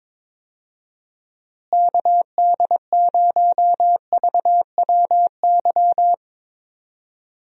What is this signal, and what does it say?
Morse code, heard as a keyed tone, about 21 wpm, 700 Hz. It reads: KD0VWY
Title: USA Call Signs - 6 Characters - Repeated 2x - 22wpm